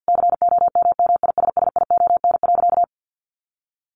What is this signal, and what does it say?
Morse code, heard as a keyed tone, about 50 wpm, 700 Hz. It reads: COMMISSION.
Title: Sentences from Top 1400 Words 50wpm